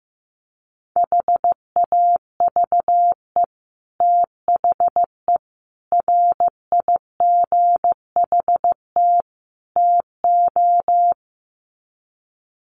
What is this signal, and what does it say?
Morse code, heard as a keyed tone, about 15 wpm, 700 Hz. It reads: HAVE THE RIGHT TO
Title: Sets of 4 Words 15wpm